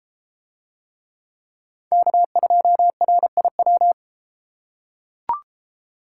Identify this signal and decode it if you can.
Morse code, heard as a keyed tone, about 29 wpm, 700 Hz. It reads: K2RIW